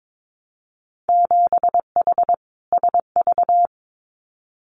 Morse code, heard as a keyed tone, about 22 wpm, 700 Hz. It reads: 7H SV
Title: Winter Field Day Exchanges Encore 22wpm